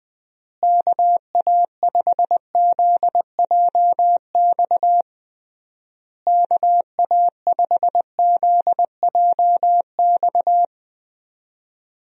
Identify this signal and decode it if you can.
Morse code, heard as a keyed tone, about 19 wpm, 700 Hz. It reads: KA5ZJX KA5ZJX
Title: USA Call Signs - 6 Characters - Repeated 3x - 20wpm